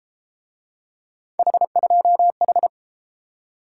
Morse code, heard as a keyed tone, about 31 wpm, 700 Hz. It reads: H2H